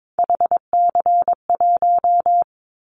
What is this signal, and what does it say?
Morse code, heard as a keyed tone, about 22 wpm, 700 Hz. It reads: HC1
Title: Callsign Prefixes for Top 100 DXCC - 22wpm